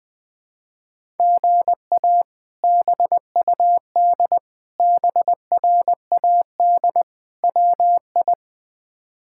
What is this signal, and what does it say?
Morse code, heard as a keyed tone, about 20 wpm, 700 Hz. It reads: GA BUD BRAD WI
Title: K1USN SST QSO Practice - 20wpm